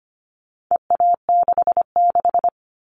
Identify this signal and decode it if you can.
Morse code, heard as a keyed tone, about 25 wpm, 700 Hz. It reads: EA66
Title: All Cut Numbers - 4-Digit 25wpm